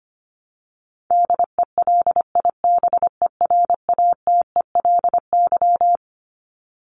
Morse code, heard as a keyed tone, about 25 wpm, 700 Hz. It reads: DELIBERATELY
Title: Top 1751-2000 Words 25wpm